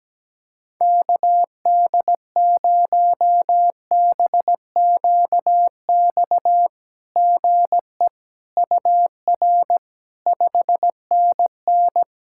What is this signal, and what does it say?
Morse code, heard as a keyed tone, about 17 wpm, 700 Hz. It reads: KD0BQX GE UR 5NN
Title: SOTA QSO Practice 17wpm